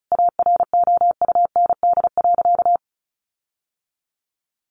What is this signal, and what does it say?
Morse code, heard as a keyed tone, about 34 wpm, 700 Hz. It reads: AROUND.
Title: Sentences from Top 1300 Words 35wpm